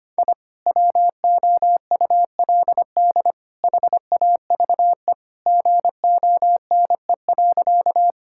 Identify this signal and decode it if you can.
Morse code, heard as a keyed tone, about 25 wpm, 700 Hz. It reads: I WOULD HAVE GONE.
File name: practice / 25wpm / Sentences with 3-Word Phrases - 25wpm